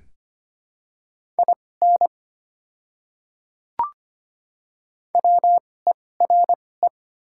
Morse code, heard as a keyed tone, about 25 wpm, spaced as for 18 wpm, 690 Hz. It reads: IN WERE
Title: 2x Character Spacing - Top 100 Words - 25wpm